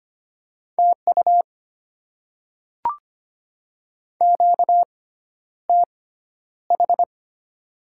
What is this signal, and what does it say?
Morse code, heard as a keyed tone, about 25 wpm, spaced as for 11 wpm, 700 Hz. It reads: TU QTH
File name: Top 20 QSO Elements - 6x Character Spacing - 25wpm